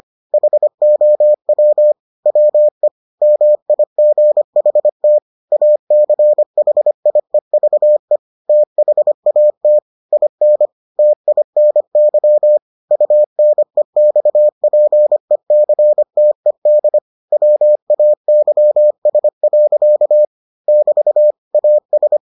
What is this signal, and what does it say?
Morse code, heard as a keyed tone, about 25 wpm, 600 Hz. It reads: HOW WE MIGHT ACHIEVE THAT IN TINY UNEXPECTED WAYS. <BT> AS